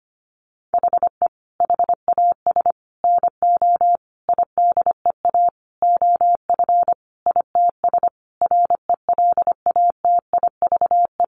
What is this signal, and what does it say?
Morse code, heard as a keyed tone, about 25 wpm, 700 Hz. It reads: HE HAS NO IDEA OF ITS RELATIVE
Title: Sentences from Top 1400 Words 25wpm